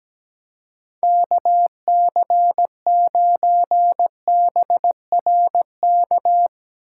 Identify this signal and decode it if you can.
Morse code, heard as a keyed tone, about 17 wpm, 700 Hz. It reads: KC9BRK
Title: USA Call Signs - 6 Characters - 3x Character Spacing - 17wpm